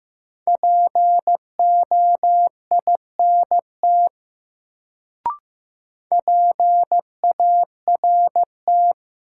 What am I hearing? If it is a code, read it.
Morse code, heard as a keyed tone, about 16 wpm, 700 Hz. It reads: POINT PART
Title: Sentences from Top 200 Words - New Word Drill - 15wpm